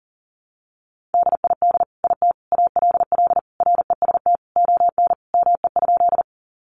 Morse code, heard as a keyed tone, about 40 wpm, 700 Hz. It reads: DID IT ALL REST ON ME?
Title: Sentences from Top 500 Words 40wpm